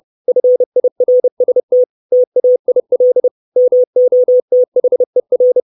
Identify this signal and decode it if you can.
Morse code, heard as a keyed tone, about 30 wpm, 500 Hz. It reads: FIRST TAIL MOTHER
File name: 632 Words 30WPM